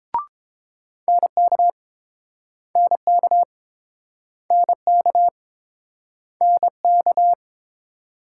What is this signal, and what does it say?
Morse code, heard as a keyed tone, about 29 wpm, 700 Hz. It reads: NK NK NK NK